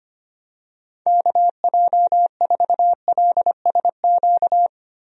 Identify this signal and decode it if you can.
Morse code, heard as a keyed tone, about 25 wpm, 700 Hz. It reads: KJ4LSQ
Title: USA Call Signs - 6 Characters - Repeated 2x - 25wpm